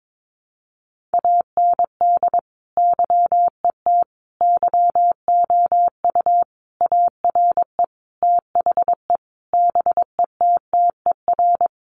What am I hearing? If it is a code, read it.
Morse code, heard as a keyed tone, about 22 wpm, 700 Hz. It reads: AND YET YOU ARE THE BETTER
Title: Sentences from Top 500 Words 22wpm